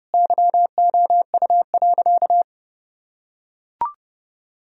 Morse code, heard as a keyed tone, about 30 wpm, 700 Hz. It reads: YOU.
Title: Sentences from Top 1000 Words - Review Part 2 - 30wpm